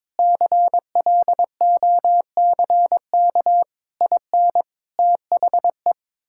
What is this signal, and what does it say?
Morse code, heard as a keyed tone, about 21 wpm, 700 Hz. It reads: CLOCK IN THE
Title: Pangram Challenge v2 - 22wpm